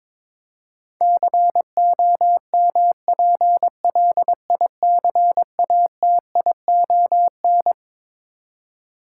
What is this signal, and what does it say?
Morse code, heard as a keyed tone, about 22 wpm, 700 Hz. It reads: COMPLICATION
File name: Twelve Letter Words - 22wpm